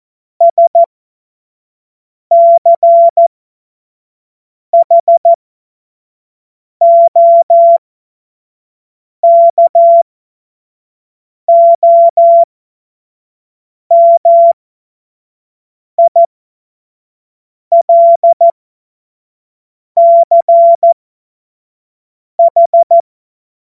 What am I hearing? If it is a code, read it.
Morse code, heard as a keyed tone, about 14 wpm, spaced as for 5 wpm, 680 Hz. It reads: SCHOKOMILCH
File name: Morsekurs / Witze / Witz 07. September 2021, 25 F70